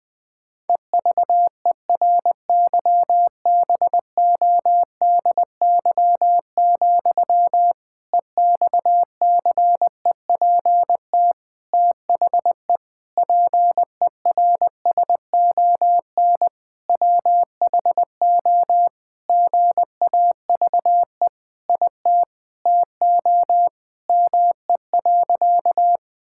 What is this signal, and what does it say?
Morse code, heard as a keyed tone, about 21 wpm, 700 Hz. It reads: EVERYBODY, EXCEPT THE PERSON WHO GAVE IT TO ME.
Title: Sentences from Top 1600 Words 20wpm